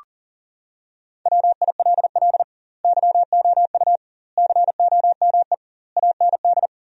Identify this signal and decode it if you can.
Morse code, heard as a keyed tone, about 40 wpm, 700 Hz. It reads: WILL YOU COME AND
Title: 2x Word-Spacing - Sentences from Top 100 Words - 40wpm